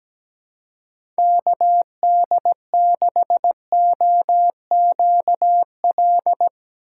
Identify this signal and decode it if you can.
Morse code, heard as a keyed tone, about 17 wpm, 700 Hz. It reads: KD6OQL
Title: USA Call Signs - 6 Characters - 6x Character Spacing - 17wpm